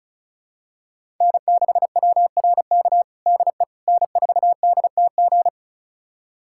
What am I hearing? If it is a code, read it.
Morse code, heard as a keyed tone, about 34 wpm, 700 Hz. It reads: N6WRK DE N4DTG
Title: USA Callsign DE Callsign 35wpm